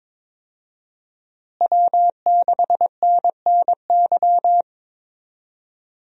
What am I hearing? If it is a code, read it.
Morse code, heard as a keyed tone, about 21 wpm, 700 Hz. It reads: W6NNY
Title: USA Call Signs - 5 Characters - Repeated 3x - 22wpm